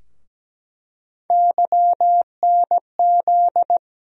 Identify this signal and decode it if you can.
Morse code, heard as a keyed tone, about 17 wpm, 700 Hz. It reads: YNZ
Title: USA Call Sign Postfixes - 3 Letters 17wpm